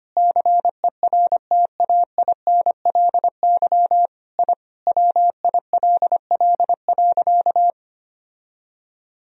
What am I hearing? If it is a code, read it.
Morse code, heard as a keyed tone, about 25 wpm, 700 Hz. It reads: CERTAINLY I WILL.
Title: Sentences from Top 1000 Words - Review Part 1 - 25wpm